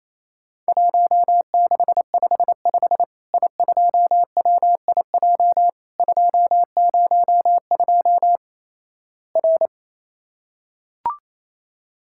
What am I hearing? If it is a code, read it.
Morse code, heard as a keyed tone, about 27 wpm, 700 Hz. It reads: 1655 I2WIJ 202 R
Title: Worked All Europe DX Contest - QTC Exchange - 28wpm